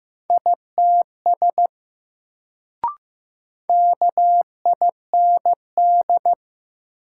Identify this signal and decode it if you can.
Morse code, heard as a keyed tone, about 15 wpm, 700 Hz. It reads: ITS KIND
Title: Sets of 2 Words - New Word Drill - 15wpm